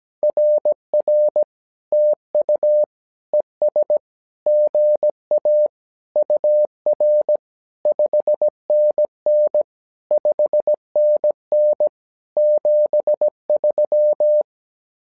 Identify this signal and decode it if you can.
Morse code, heard as a keyed tone, about 17 wpm, 600 Hz. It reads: RR TU ES GA UR 5NN 5NN 73
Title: SOTA QSO Practice 17wpm